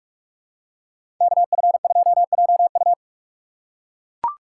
Morse code, heard as a keyed tone, about 41 wpm, 700 Hz. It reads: KF2JU
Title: USA Call Signs - 5 Characters - 3x Character Spacing - 45wpm